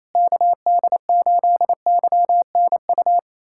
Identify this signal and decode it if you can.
Morse code, heard as a keyed tone, about 28 wpm, 700 Hz. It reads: KD8YNU
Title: USA Callsign DE Callsign - 6 Characters - 28wpm